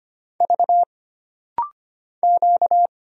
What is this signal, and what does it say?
Morse code, heard as a keyed tone, about 25 wpm, 700 Hz. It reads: V Q
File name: Single Character (Letter:Number:Punctuation:Prosign) - Rapid-Fire - 25wpm